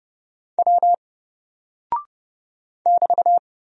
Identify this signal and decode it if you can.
Morse code, heard as a keyed tone, about 30 wpm, 700 Hz. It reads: W <BT>
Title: Single Character (Letter:Number:Punctuation:Prosign) 30wpm